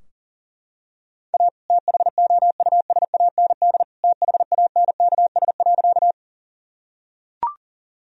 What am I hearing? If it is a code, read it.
Morse code, heard as a keyed tone, about 40 wpm, 700 Hz. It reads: A THOUSAND THANKS.
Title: Sentences from Top 1000 Words - Review Part 3 - 40wpm